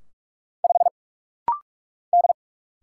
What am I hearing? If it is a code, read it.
Morse code, heard as a keyed tone, about 46 wpm, 690 Hz. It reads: 5 D
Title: Single Letter-Number - Rapid-Fire 45wpm